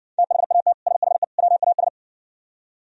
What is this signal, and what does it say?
Morse code, heard as a keyed tone, about 59 wpm, 700 Hz. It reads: THAT SHE WAS